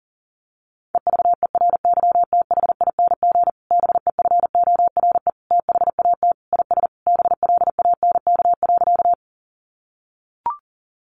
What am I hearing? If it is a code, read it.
Morse code, heard as a keyed tone, about 40 wpm, 700 Hz. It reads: EVERYTHING BEFORE THAT IS BLANK.
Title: Sentences from Top 1500 Words 40wpm